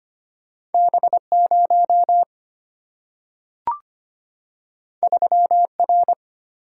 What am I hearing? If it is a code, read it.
Morse code, heard as a keyed tone, about 25 wpm, 690 Hz. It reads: B0 3R
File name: Two Letter-Number Combinations - 25wpm